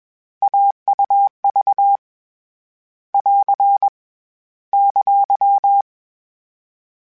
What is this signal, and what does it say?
Morse code, heard as a keyed tone, about 21 wpm, 800 Hz. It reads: AUV <AR> !